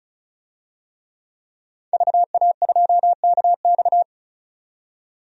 Morse code, heard as a keyed tone, about 34 wpm, 700 Hz. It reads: VA2KX
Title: DXCC-100 Call Signs - 5 Characters - 35wpm